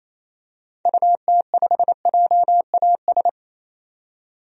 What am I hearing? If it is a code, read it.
Morse code, heard as a keyed tone, about 27 wpm, 700 Hz. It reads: UT5JAS
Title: DXCC-100 Call Signs - 6 Characters - 28wpm